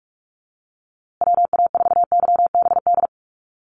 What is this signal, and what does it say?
Morse code, heard as a keyed tone, about 44 wpm, 700 Hz. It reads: WA4YBD
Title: USA Call Signs Repeated 3x 45wpm